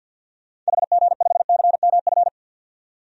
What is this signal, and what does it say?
Morse code, heard as a keyed tone, about 47 wpm, 700 Hz. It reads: HG5CMF
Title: DXCC-100 Call Signs - 6 Characters - 50wpm